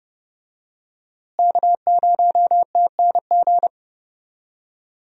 Morse code, heard as a keyed tone, about 30 wpm, 700 Hz. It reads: K0TNG
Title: USA Call Signs - 5 Characters - Repeated 2x - 30wpm